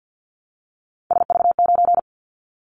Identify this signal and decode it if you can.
Morse code, heard as a keyed tone, about 48 wpm, 700 Hz. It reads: SV9
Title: Callsign Prefixes for Top 100 DXCC - 50wpm